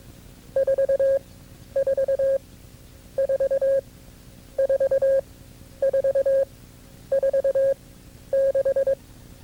Morse code, heard as a keyed tone, about 22 wpm, spaced as for 11 wpm, 540 Hz. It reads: 44 4 4446
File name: CW-practice/KB6MT-tape2-side1